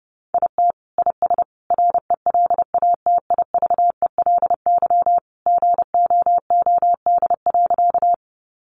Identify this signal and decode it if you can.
Morse code, heard as a keyed tone, about 30 wpm, 700 Hz. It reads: IT IS RELATIVELY GOOD.
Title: Sentences from Top 1100 Words 30wpm